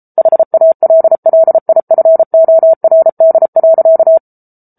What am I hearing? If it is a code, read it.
Morse code, heard as a keyed tone, about 34 wpm, 650 Hz. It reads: HALLIFORD.
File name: Book Two / 07-Chapter 7 - 33 WPM